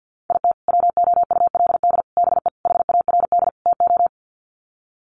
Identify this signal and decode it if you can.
Morse code, heard as a keyed tone, about 50 wpm, 700 Hz. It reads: IT WOULD BE HARD TO
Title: Sets of 5 Words Encore2 50wpm